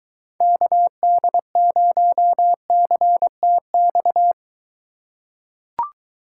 Morse code, heard as a keyed tone, about 23 wpm, 700 Hz. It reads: KD0CTX